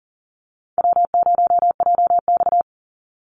Morse code, heard as a keyed tone, about 40 wpm, 700 Hz. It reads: W0JX
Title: Call Signs Repeated 3x - 40wpm